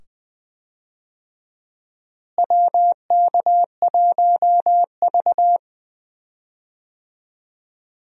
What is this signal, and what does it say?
Morse code, heard as a keyed tone, about 20 wpm, 700 Hz. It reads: WK1V
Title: USA Call Signs - 4 Characters - Repeated 2x - 20wpm